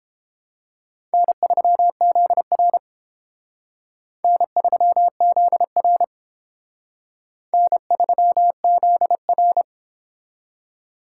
Morse code, heard as a keyed tone, about 29 wpm, 700 Hz. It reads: N3ZR N3ZR N3ZR